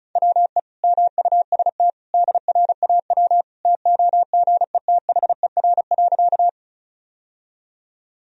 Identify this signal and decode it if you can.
Morse code, heard as a keyed tone, about 34 wpm, 700 Hz. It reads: WE MUST DRAW TOGETHER.
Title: Sentences from Top 1000 Words - Review Part 1 - 35wpm